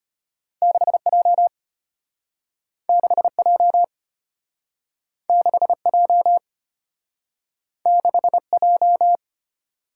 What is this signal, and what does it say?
Morse code, heard as a keyed tone, about 32 wpm, 700 Hz. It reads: 6J 6J 6J 6J